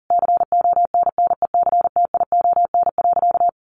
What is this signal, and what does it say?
Morse code, heard as a keyed tone, about 40 wpm, 700 Hz. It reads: CONNECTION.
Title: Sentences from Top 1300 Words 40wpm